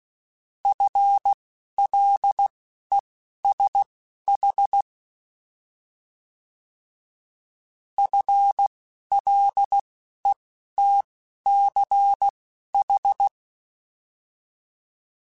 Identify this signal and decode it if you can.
Morse code, heard as a keyed tone, about 16 wpm, spaced as for 11 wpm, 800 Hz. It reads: FLESH FLETCH
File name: Alphabetical / Alphabetical mp3s / Fs @10